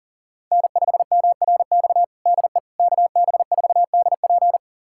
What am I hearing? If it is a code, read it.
Morse code, heard as a keyed tone, about 40 wpm, 700 Hz. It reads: N5MRX DE KB4DP
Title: USA Callsign DE Callsign - 5 Characters - 40wpm